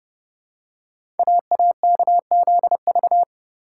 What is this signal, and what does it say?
Morse code, heard as a keyed tone, about 30 wpm, 700 Hz. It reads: AAKZV
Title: Code Groups - 5 Letter:Numbers - 30wpm